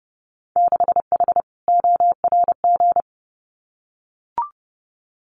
Morse code, heard as a keyed tone, about 30 wpm, 700 Hz. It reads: 6H ORG
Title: Winter Field Day Exchanges Encore 30wpm